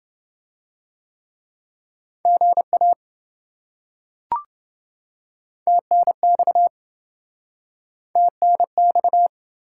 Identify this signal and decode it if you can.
Morse code, heard as a keyed tone, about 27 wpm, 700 Hz. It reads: GA TNX TNX